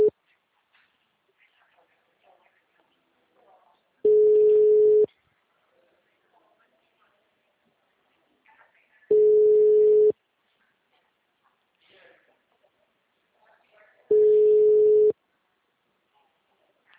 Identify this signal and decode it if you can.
Ringback tone (Europe, 425 Hz), with speech.
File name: out- -2006- - - .82969